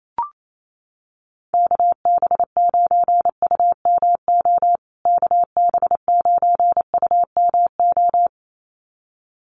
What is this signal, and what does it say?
Morse code, heard as a keyed tone, about 28 wpm, 700 Hz. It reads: KB9UMO KB9UMO
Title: USA Call Signs Repeated 2x 28wpm